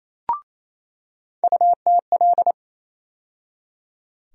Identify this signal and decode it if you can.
Morse code, heard as a keyed tone, about 27 wpm, 700 Hz. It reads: UTL
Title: USA Call Sign Postfixes - 3 Letters 28wpm